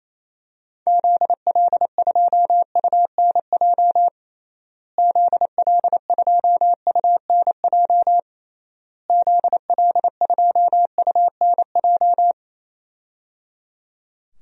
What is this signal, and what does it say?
Morse code, heard as a keyed tone, about 28 wpm, 700 Hz. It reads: ZL2UNJ ZL2UNJ ZL2UNJ
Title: DXCC-100 Call Signs - 6 Characters - Repeated 3x - 28wpm